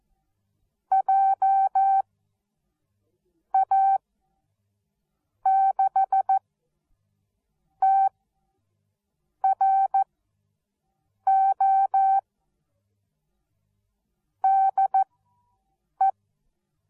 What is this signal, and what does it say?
Morse code, heard as a keyed tone, about 14 wpm, spaced as for 5 wpm, 780 Hz. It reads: JA6TRO DE